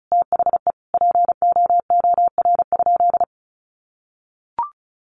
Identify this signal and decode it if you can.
Morse code, heard as a keyed tone, about 34 wpm, 700 Hz. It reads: THE POOR?
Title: Sentences from Top 1100 Words 35wpm